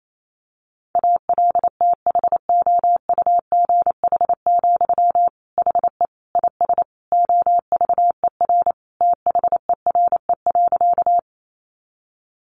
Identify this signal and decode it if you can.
Morse code, heard as a keyed tone, about 28 wpm, 700 Hz. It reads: ALTHOUGH, HE IS OVER THERE.
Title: Sentences from Top 600 Words 28wpm